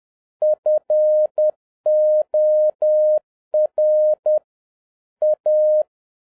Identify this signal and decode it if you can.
Morse code, heard as a keyed tone, about 10 wpm, 610 Hz. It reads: FOR A
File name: 2011-07-mp3 / n41 EXCITING QRP CW OPERATION 10wpm